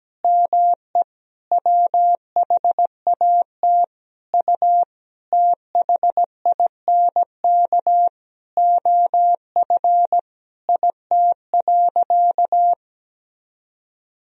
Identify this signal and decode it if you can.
Morse code, heard as a keyed tone, about 17 wpm, 700 Hz. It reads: ME WHAT U THINK OF IT.
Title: Sentences from Top 200 Words and Abbreviations 17wpm